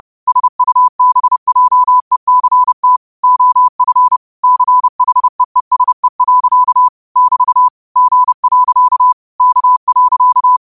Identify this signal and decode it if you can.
Morse code, heard as a keyed tone, about 30 wpm, 1000 Hz. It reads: SUBJECT OF CHEESE. <BT> G. K.
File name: qotdfiles / 30wpm